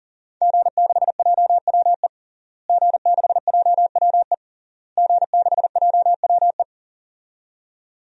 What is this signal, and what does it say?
Morse code, heard as a keyed tone, about 40 wpm, 700 Hz. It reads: G6JWE G6JWE G6JWE
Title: DXCC-100 Call Signs - 5 Characters - Repeated 3x - 40wpm